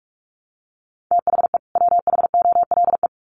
Morse code, heard as a keyed tone, about 44 wpm, 700 Hz. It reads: THE WHOLE
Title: Sets of 2 Words 45wpm